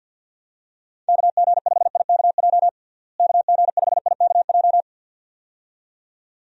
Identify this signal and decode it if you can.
Morse code, heard as a keyed tone, about 51 wpm, 700 Hz. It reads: KG5IKJ KG5IKJ
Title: USA Call Signs - 6 Characters - Repeated 2x - 50wpm